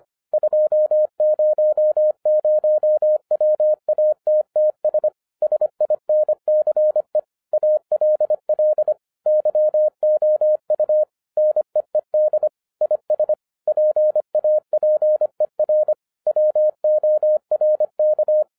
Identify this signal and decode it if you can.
Morse code, heard as a keyed tone, about 26 wpm, 610 Hz. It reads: 200 WATTS SINCE ALL YOU NEED IS PAPER WORK